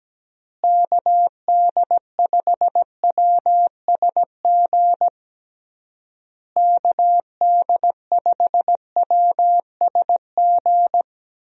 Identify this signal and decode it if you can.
Morse code, heard as a keyed tone, about 17 wpm, 700 Hz. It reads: KD5WSG KD5WSG
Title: USA Call Signs - 6 Characters - Repeated 3x - 17wpm